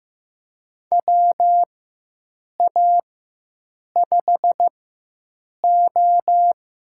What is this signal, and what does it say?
Morse code, heard as a keyed tone, about 15 wpm, spaced as for 7 wpm, 700 Hz. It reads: WA5O
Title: USA Call Signs - 6 Characters - 4x Character Spacing - 15wpm